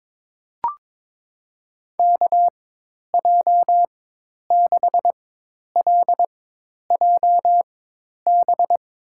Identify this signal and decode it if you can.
Morse code, heard as a keyed tone, about 22 wpm, spaced as for 10 wpm, 700 Hz. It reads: KJ6LJB
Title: USA Call Signs - 6 Characters - 4x Character Spacing - 22wpm